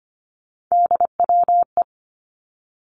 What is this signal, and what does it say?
Morse code, heard as a keyed tone, about 25 wpm, 700 Hz. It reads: DWE
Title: Three Challenging Letter-Number Combinations 25wpm